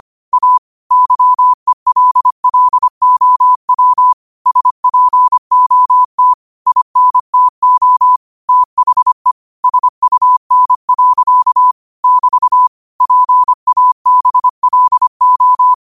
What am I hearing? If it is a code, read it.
Morse code, heard as a keyed tone, about 25 wpm, 1000 Hz. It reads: A YELLOW SPOT INTO THE SUN. <BT> PABLO
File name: qotdfiles / 25wpm